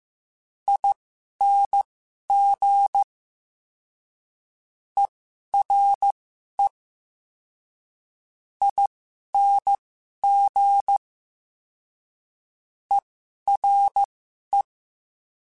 Morse code, heard as a keyed tone, about 15 wpm, spaced as for 11 wpm, 800 Hz. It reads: ING ERE ING ERE
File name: Special INGs-mp3s / Special INGs-mp3s / Ere ing @10